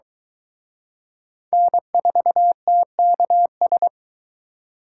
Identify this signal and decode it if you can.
Morse code, heard as a keyed tone, about 24 wpm, 700 Hz. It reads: N4TKS